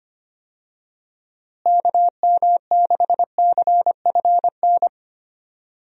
Morse code, heard as a keyed tone, about 25 wpm, 700 Hz. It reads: KM6CFN
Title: USA Call Signs - 6 Characters - Repeated 2x - 25wpm